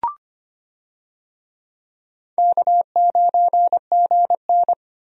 Morse code, heard as a keyed tone, about 25 wpm, 700 Hz. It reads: K9GN
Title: USA Call Signs - 4 Characters - Repeated 3x - 25wpm